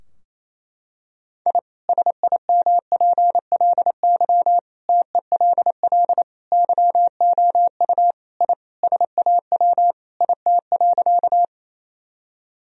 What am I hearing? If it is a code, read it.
Morse code, heard as a keyed tone, about 28 wpm, 700 Hz. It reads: I SIMPLY TELL YOU I SAW IT.
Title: Sentences from Top 1000 Words - Review Part 4 - 28wpm